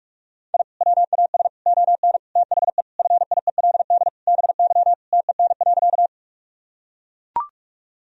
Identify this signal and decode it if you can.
Morse code, heard as a keyed tone, about 44 wpm, 700 Hz. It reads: I WAS ON THE FIELD BY TEN.
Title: Sentences from Top 1000 Words - Review Part 2 - 45wpm